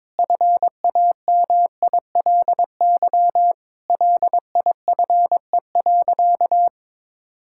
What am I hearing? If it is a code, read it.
Morse code, heard as a keyed tone, about 22 wpm, 700 Hz. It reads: FAMILY LIFE.
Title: Sentences from Top 1000 Words - Review Part 5 - 22wpm